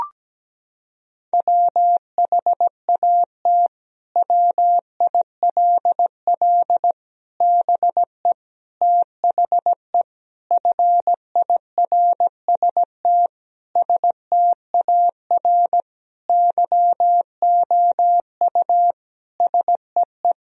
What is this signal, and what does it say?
Morse code, heard as a keyed tone, about 17 wpm, 700 Hz. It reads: WHAT WILL BE THE FIRST STAR YOU SEE
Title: Sentences from Top 1000 Words - Review Part 2 - 17wpm